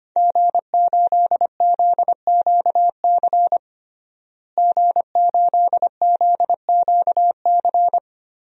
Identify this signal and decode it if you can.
Morse code, heard as a keyed tone, about 25 wpm, 700 Hz. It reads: G8ZQC G8ZQC
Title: DXCC-100 Call Signs - 5 Characters - Repeated 3x - 25wpm